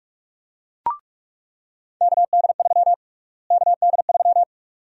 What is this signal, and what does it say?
Morse code, heard as a keyed tone, about 46 wpm, 700 Hz. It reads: KD3 KD3
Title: USA Call Sign Prefixes - Repeated 2x - 45wpm